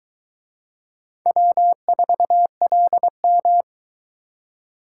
Morse code, heard as a keyed tone, about 22 wpm, 700 Hz. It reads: W4LM